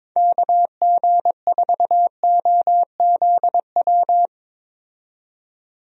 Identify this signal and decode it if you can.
Morse code, heard as a keyed tone, about 22 wpm, 700 Hz. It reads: KG4OZW
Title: USA Call Signs - 6 Characters - 22wpm